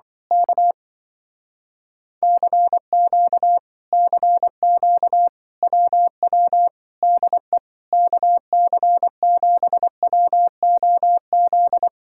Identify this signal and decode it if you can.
Morse code, heard as a keyed tone, about 24 wpm, 700 Hz. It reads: K CQ CQ WW DE KC7WOZ
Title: Speed-Racing Common QSO Phrases - 20wpm